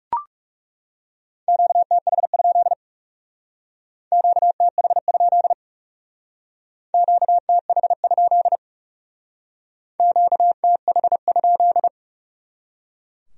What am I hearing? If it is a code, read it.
Morse code, heard as a keyed tone, about 40 wpm, 700 Hz. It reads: QTH? QTH? QTH? QTH?